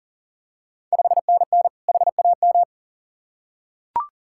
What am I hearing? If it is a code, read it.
Morse code, heard as a keyed tone, about 40 wpm, 690 Hz. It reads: 5NN HAM
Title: Ontario QSO Party Exchange 40wpm